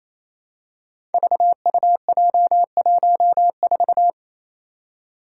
Morse code, heard as a keyed tone, about 29 wpm, 700 Hz. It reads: VUJ14
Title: Five Challenging Letter-Number Combinations 28wpm